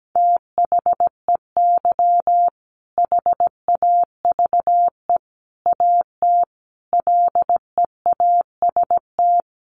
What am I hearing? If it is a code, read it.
Morse code, heard as a keyed tone, about 17 wpm, 700 Hz. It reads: THEY HAVE AT LEAST
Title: Sentences from Top 900 Words 17wpm